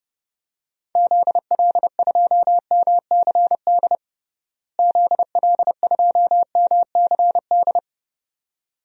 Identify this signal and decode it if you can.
Morse code, heard as a keyed tone, about 32 wpm, 700 Hz. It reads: ZL2MCD ZL2MCD
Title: DXCC-100 Call Signs - 6 Characters - Repeated 2x - 30wpm